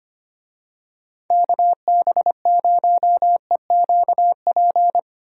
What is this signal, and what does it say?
Morse code, heard as a keyed tone, about 25 wpm, 700 Hz. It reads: KB0EQP
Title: USA Call Signs - 6 Characters - Repeated 2x - 25wpm